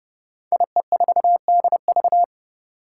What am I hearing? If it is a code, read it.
Morse code, heard as a keyed tone, about 30 wpm, 700 Hz. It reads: IE4DV
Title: Five Challenging Letter-Number Combinations 30wpm